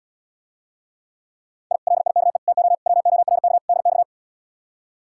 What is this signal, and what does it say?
Morse code, heard as a keyed tone, about 79 wpm, 700 Hz. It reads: I LIKE TO WORK DX